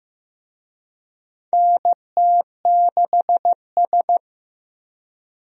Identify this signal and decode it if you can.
Morse code, heard as a keyed tone, about 15 wpm, 700 Hz. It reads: NT6S
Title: USA Call Signs - 4 Characters - 15wpm